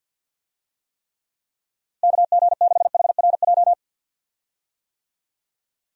Morse code, heard as a keyed tone, about 52 wpm, 700 Hz. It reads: KG6HRJ
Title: USA Call Signs - 6 Characters - Repeated 3x - 50wpm